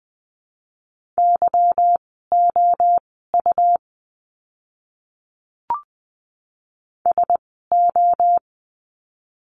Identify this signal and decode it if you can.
Morse code, heard as a keyed tone, about 20 wpm, spaced as for 14 wpm, 700 Hz. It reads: YOU SO
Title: 230 Lesson 37f - Letter x - Words 20wpm